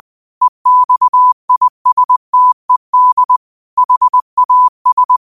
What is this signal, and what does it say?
Morse code, heard as a keyed tone, about 20 wpm, 1000 Hz. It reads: EXISTED HAS
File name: qotdfiles / 20wpm